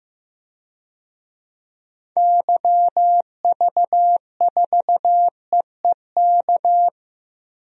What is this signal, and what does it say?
Morse code, heard as a keyed tone, about 15 wpm, 700 Hz. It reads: YV4EEK
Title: DXCC-100 Call Signs - 6 Characters - Repeated 2x - 15wpm